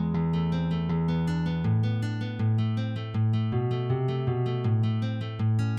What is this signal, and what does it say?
There is no code.